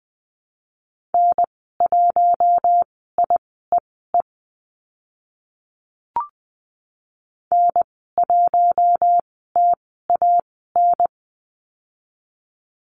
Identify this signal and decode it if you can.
Morse code, heard as a keyed tone, about 20 wpm, spaced as for 14 wpm, 700 Hz. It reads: N1IEE N1TAN
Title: 048 Lesson 9g - Number 4 - Call Signs 20wpm